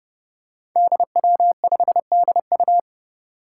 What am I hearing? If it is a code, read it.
Morse code, heard as a keyed tone, about 30 wpm, 700 Hz. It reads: DW5DU
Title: Five Challenging Letter-Number Combinations 30wpm